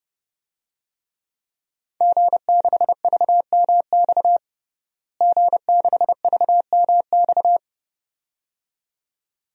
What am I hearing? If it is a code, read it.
Morse code, heard as a keyed tone, about 30 wpm, 700 Hz. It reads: G6VMX G6VMX
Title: DXCC-100 Call Signs - 5 Characters - Repeated 2x - 30wpm